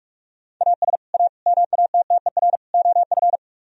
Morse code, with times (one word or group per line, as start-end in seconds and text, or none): 0.6-1.0 AS
1.1-1.3 A
1.5-2.6 MATTER
2.7-3.4 OF